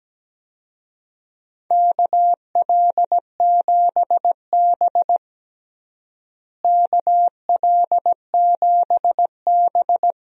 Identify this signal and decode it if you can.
Morse code, heard as a keyed tone, about 17 wpm, 700 Hz. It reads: KL7B KL7B
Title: DXCC-100 Call Signs - 4 Characters - Repeated 3x - 17wpm